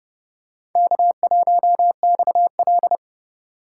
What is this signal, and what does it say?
Morse code, heard as a keyed tone, about 30 wpm, 700 Hz. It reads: K1XL
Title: USA Call Signs - 4 Characters - 3x Character Spacing - 30wpm